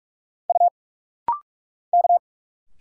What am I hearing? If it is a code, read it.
Morse code, heard as a keyed tone, about 46 wpm, 690 Hz. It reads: U K
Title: Single Letters - Rapid-Fire 45wpm